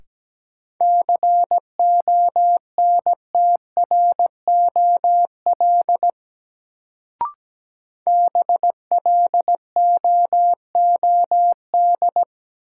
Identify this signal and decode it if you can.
Morse code, heard as a keyed tone, about 17 wpm, 700 Hz. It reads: CONTROL BLOOD
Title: Sentences from Top 1000 Words - Review Part 2 - New Word Drill - 17wpm